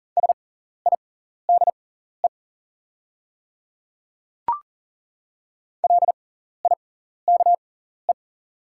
Morse code, heard as a keyed tone, about 40 wpm, spaced as for 14 wpm, 700 Hz. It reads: SIDE LIKE
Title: 6x Character Spacing - Top 100 Words - 40wpm